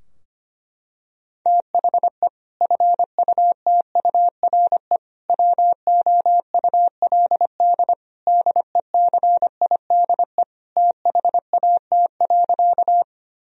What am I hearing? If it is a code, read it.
Morse code, heard as a keyed tone, about 25 wpm, 700 Hz. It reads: THE FUTURE WOULD DECIDE THAT.
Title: Sentences from Top 1000 Words - Review Part 3 - 25wpm